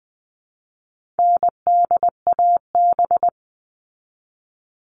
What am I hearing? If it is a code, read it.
Morse code, heard as a keyed tone, about 20 wpm, 700 Hz. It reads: NDAB
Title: All Cut Numbers - 4-Digit 20wpm